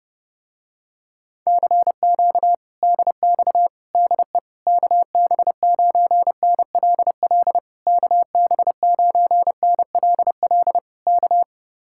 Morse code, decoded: CQ DX DE KB9NLL KB9NLL K

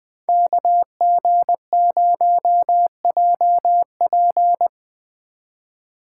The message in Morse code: KG0JP

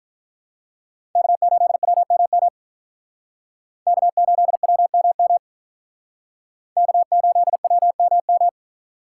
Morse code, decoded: K8WMM K8WMM K8WMM

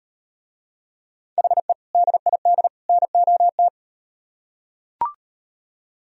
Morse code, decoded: HE DID NOT

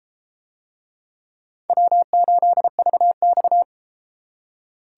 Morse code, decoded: W8VX

